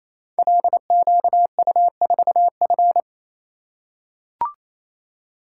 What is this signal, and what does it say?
Morse code, heard as a keyed tone, about 28 wpm, 700 Hz. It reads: LQU4F
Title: Code Groups - 5 Letter:Numbers - 28wpm